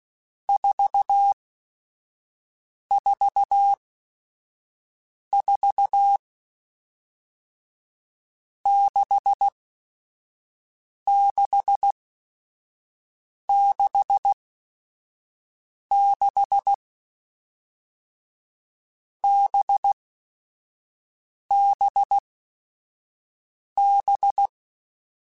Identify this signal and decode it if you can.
Morse code, heard as a keyed tone, about 16 wpm, 800 Hz. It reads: 4 4 4 6 6 6 6 B B B
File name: Groups@5 / Groups@5 mp3s / VB46@5